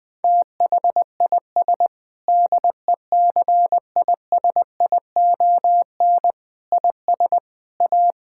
Morse code, heard as a keyed tone, about 20 wpm, 700 Hz. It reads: THIS DECISION IS A
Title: Sentences with 8-Word Phrases - 20wpm